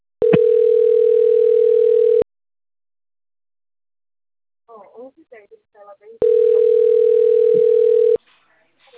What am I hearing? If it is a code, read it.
Ringback tone (North America), 440 + 480 Hz, with speech.
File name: out- - - - - .10862